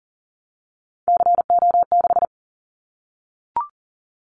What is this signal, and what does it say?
Morse code, heard as a keyed tone, about 40 wpm, 700 Hz. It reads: CO6
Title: Callsign Prefixes for Top 100 DXCC - 40wpm